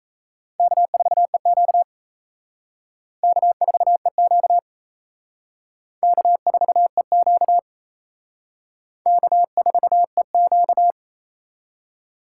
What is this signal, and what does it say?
Morse code, heard as a keyed tone, about 35 wpm, 700 Hz. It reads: K4EQ K4EQ K4EQ K4EQ
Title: Speed Racing - USA Call Signs 28wpm